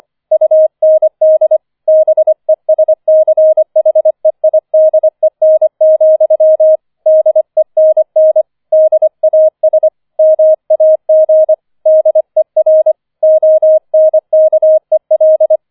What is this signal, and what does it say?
Morse code, heard as a keyed tone, about 25 wpm, 630 Hz. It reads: UND BESCHEIDEN, DENN DAS MAG DER ONKEL
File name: 005 Fünfter Streich 25wpm